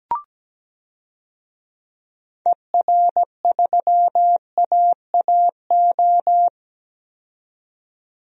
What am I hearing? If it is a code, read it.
Morse code, heard as a keyed tone, about 17 wpm, 700 Hz. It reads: ER3AAO
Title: DXCC-100 Call Signs - 6 Characters - 17wpm